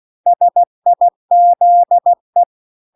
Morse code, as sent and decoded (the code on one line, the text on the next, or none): ... .. --.. .
SIZE